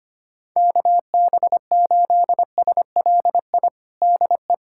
Morse code, heard as a keyed tone, about 25 wpm, 700 Hz. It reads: KB8SLI DE
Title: USA Callsign DE Callsign - 6 Characters - 25wpm